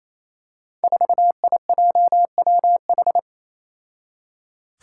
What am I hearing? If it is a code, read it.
Morse code, heard as a keyed tone, about 28 wpm, 700 Hz. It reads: 4IJWH
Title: Five Challenging Letter-Number Combinations 28wpm